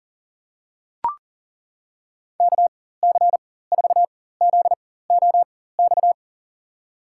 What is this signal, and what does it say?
Morse code, heard as a keyed tone, about 40 wpm, spaced as for 19 wpm, 700 Hz. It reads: KC4ZOX
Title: USA Call Signs - 6 Characters - 4x Character Spacing - 40wpm